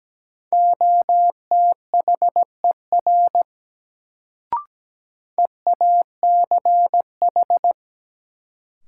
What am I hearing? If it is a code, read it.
Morse code, heard as a keyed tone, about 17 wpm, 700 Hz. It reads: OTHER EACH
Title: Sets of 2 Words - New Word Drill - 17wpm